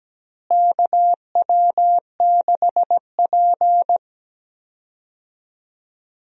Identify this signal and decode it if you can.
Morse code, heard as a keyed tone, about 17 wpm, 700 Hz. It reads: KW6P
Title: USA Call Signs - 4 Characters - Repeated 2x - 17wpm